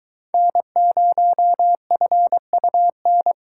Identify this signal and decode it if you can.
Morse code, heard as a keyed tone, about 21 wpm, 700 Hz. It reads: N0FUN